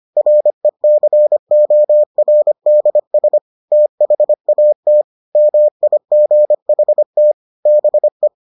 Morse code, heard as a keyed tone, about 25 wpm, 600 Hz. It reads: RECORDS THAT MIGHT BE